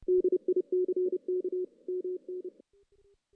Morse code, heard as a keyed tone, about 28 wpm, 370 Hz. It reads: DICK MN
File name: 2016 JAN CW NAQP / 2016 Jan naqp k5ot - 0-5